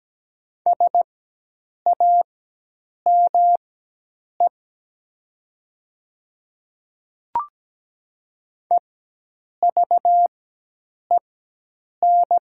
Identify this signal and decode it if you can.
Morse code, heard as a keyed tone, about 17 wpm, spaced as for 8 wpm, 700 Hz. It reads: SAME EVEN
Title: 4x Character Spacing - Top 200 Words - 17wpm